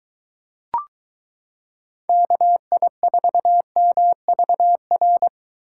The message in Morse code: KI4MVR

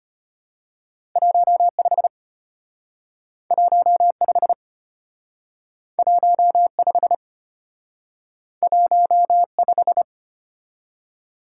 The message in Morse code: 15 15 15 15

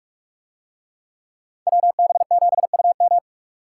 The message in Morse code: WB7UM

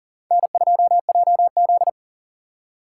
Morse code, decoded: N2JZ

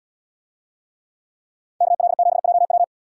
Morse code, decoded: KL7JW

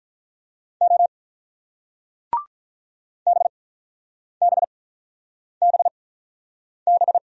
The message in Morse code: O BBBB

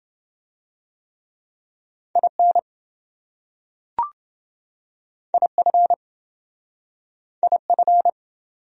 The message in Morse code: IN IF IF

